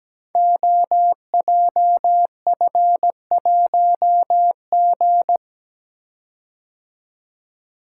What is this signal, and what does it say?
Morse code, heard as a keyed tone, about 17 wpm, 700 Hz. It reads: OJF1G
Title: Code Groups - 5 Letter:Numbers - 17wpm